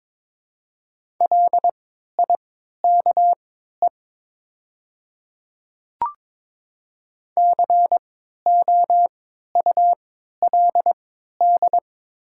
Morse code, heard as a keyed tone, about 23 wpm, spaced as for 13 wpm, 700 Hz. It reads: LIKE COULD
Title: 3x Character Spacing - Top 200 Words - 22wpm